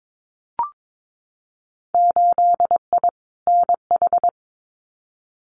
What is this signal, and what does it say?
Morse code, heard as a keyed tone, about 22 wpm, 700 Hz. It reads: 8I NH
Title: Winter Field Day Exchanges Encore 22wpm